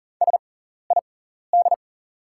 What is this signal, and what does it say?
Morse code, heard as a keyed tone, about 40 wpm, spaced as for 14 wpm, 690 Hz. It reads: SID
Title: 6x Character Spacing - Top 200 Words - 40wpm